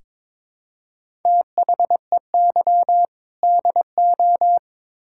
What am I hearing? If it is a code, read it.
Morse code, heard as a keyed tone, about 21 wpm, 700 Hz. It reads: THEY DO
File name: practice / 22wpm / Build A Phrase - 5-Word Phrases - 22wpm